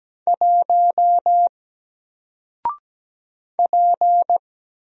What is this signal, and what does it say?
Morse code, heard as a keyed tone, about 17 wpm, 700 Hz. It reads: 1 P
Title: Single Letter-Number 17wpm